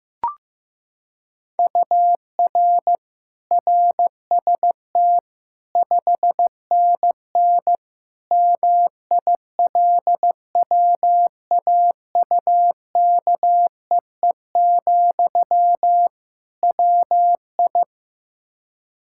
Morse code, heard as a keyed tone, about 15 wpm, 700 Hz. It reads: UR RST 5NN MILWAUKEE, WI
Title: K1USN SST QSO Practice 15wpm